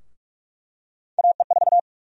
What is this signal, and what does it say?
Morse code, heard as a keyed tone, about 46 wpm, 700 Hz. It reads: AE4
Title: USA Call Sign Prefixes - Repeated 2x - 45wpm